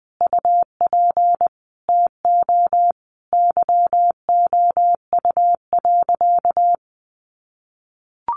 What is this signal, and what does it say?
Morse code, heard as a keyed tone, about 20 wpm, 700 Hz. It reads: UP TO YOU.
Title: Sentences from Top 1200 Words 20wpm